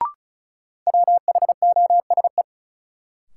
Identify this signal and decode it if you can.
Morse code, heard as a keyed tone, about 34 wpm, 700 Hz. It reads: WHOSE
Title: Sentences from Top 1200 Words - New Word Drill - 35wpm